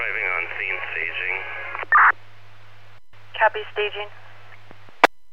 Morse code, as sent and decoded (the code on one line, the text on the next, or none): none